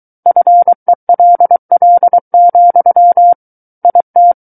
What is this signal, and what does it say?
Morse code, heard as a keyed tone, about 23 wpm, 700 Hz. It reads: FELL, IT
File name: Book One / 14-Chapter 14 - 23 WPM